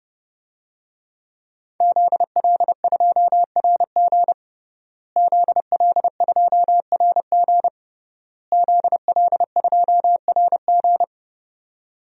Morse code, heard as a keyed tone, about 30 wpm, 700 Hz. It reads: ZL2RG ZL2RG ZL2RG
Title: DXCC-100 Call Signs - 5 Characters - Repeated 3x - 30wpm